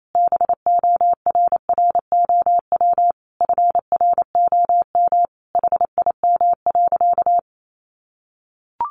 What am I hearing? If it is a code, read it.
Morse code, heard as a keyed tone, about 28 wpm, 700 Hz. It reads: BORROW FROM HIM.
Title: Sentences from Top 1500 Words 28wpm